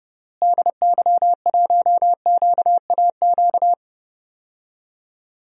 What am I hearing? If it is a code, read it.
Morse code, heard as a keyed tone, about 30 wpm, 700 Hz. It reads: DY1QAQ
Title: DXCC-100 Call Signs - 6 Characters - Repeated 2x - 30wpm